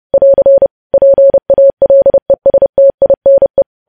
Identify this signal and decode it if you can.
Morse code, heard as a keyed tone, about 30 wpm, 550 Hz. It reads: <AR> PALESTINE
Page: news headlines in morse code at 30 wpm